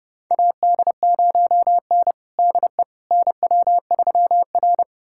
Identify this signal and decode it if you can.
Morse code, heard as a keyed tone, about 30 wpm, 700 Hz. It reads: AD0N DE NW3R
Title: USA Callsign DE Callsign - 4 Characters - 30wpm